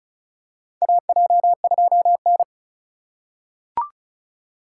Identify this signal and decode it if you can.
Morse code, heard as a keyed tone, about 36 wpm, 700 Hz. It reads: AJ2N
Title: USA Call Signs - 4 Characters - 3x Character Spacing - 35wpm